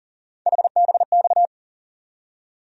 Morse code, heard as a keyed tone, about 40 wpm, 700 Hz. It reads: HBX
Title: USA Call Sign Postfixes - 3 Letters 40wpm